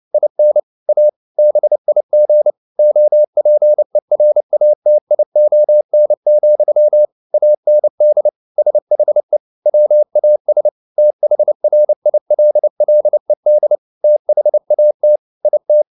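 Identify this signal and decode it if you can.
Morse code, heard as a keyed tone, about 29 wpm, 600 Hz. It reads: IN A BIG OPERATION, AND SHE WAS THRILLED THAT IT
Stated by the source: Morse code at 29 wpm, sending Ken Follett's novel Never in English, from CW med Gnister.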